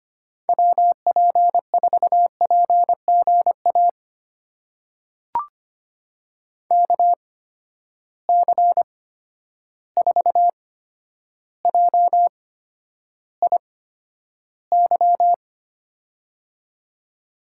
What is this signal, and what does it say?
Morse code, heard as a keyed tone, about 25 wpm, spaced as for 11 wpm, 700 Hz. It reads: WP4PGA KC4JIY
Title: USA Call Signs - 6 Characters - 8x Character Spacing - 25wpm